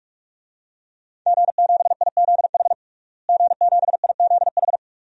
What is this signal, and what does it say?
Morse code, heard as a keyed tone, about 43 wpm, 700 Hz. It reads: G7IZH G7IZH
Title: DXCC-100 Call Signs - 5 Characters - Repeated 2x - 45wpm